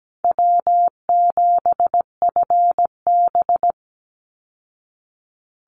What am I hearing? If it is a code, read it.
Morse code, heard as a keyed tone, about 17 wpm, 700 Hz. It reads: W7FB
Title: Easy Call Signs Repeated 2x - 17wpm